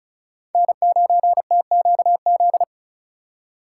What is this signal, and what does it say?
Morse code, heard as a keyed tone, about 36 wpm, 700 Hz. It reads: N9TQZ
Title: USA Call Signs - 5 Characters - 3x Character Spacing - 35wpm